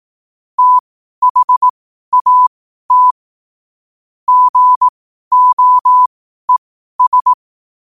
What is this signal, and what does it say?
Morse code, heard as a keyed tone, about 18 wpm, spaced as for 12 wpm, 1000 Hz. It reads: THAT GOES
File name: qotdfiles / 12wpm